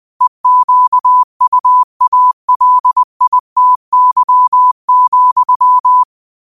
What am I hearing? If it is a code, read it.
Morse code, heard as a keyed tone, about 20 wpm, 1000 Hz. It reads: EQUALITY,